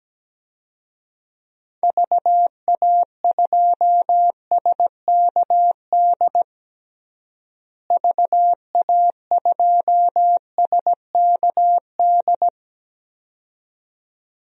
Morse code, heard as a keyed tone, about 17 wpm, 700 Hz. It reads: VA2SKD VA2SKD
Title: DXCC-100 Call Signs - 6 Characters - Repeated 2x - 17wpm